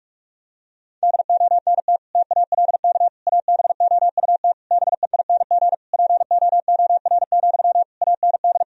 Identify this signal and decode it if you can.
Morse code, heard as a keyed tone, about 46 wpm, 700 Hz. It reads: DONT TALK ABOUT BEING POOR, AND
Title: Sentences from Top 1000 Words - Review Part 4 - 45wpm